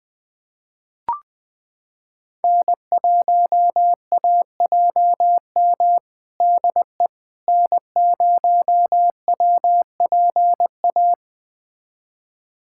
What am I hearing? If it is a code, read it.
Morse code, heard as a keyed tone, about 19 wpm, 700 Hz. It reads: N1AJM DE N0WPA
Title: USA Callsign DE Callsign 20wpm